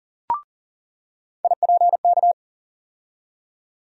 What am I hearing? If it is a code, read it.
Morse code, heard as a keyed tone, about 38 wpm, 700 Hz. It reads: IPK
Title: USA Call Sign Postfixes - 3 Letters 40wpm